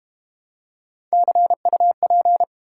Morse code, heard as a keyed tone, about 32 wpm, 690 Hz. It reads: CUP